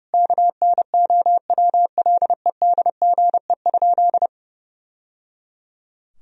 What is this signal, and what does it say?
Morse code, heard as a keyed tone, about 30 wpm, 700 Hz. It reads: KNOWLEDGE?
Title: Sentences from Top 1000 Words - Review Part 3 - 30wpm